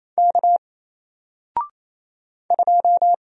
Morse code, heard as a keyed tone, about 27 wpm, 700 Hz. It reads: K 2
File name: Single-Letter-Number-28wpm